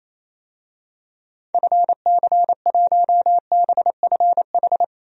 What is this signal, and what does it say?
Morse code, heard as a keyed tone, about 28 wpm, 700 Hz. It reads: FC1BFH
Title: DXCC-100 Call Signs - 6 Characters - 28wpm